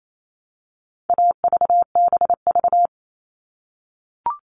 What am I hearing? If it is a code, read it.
Morse code, heard as a keyed tone, about 28 wpm, 700 Hz. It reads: AVBV
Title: All Cut Numbers - 4-Digit 28wpm